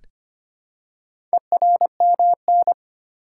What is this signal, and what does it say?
Morse code, heard as a keyed tone, about 25 wpm, 700 Hz. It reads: ERMN
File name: practice / 25wpm / Four Letter-Number Combinations - 25wpm